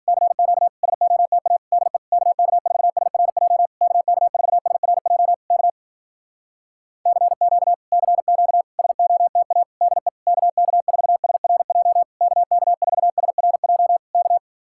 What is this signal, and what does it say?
Morse code, heard as a keyed tone, about 59 wpm, 700 Hz. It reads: CQ SOTA DE KK4SRJ KK4SRJ K CQ CQ SOTA DE KK4SRJ KK4SRJ K